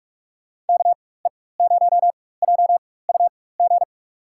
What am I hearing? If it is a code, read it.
Morse code, heard as a keyed tone, about 46 wpm, spaced as for 21 wpm, 700 Hz. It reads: KE0JUG
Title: USA Call Signs - 6 Characters - 4x Character Spacing - 45wpm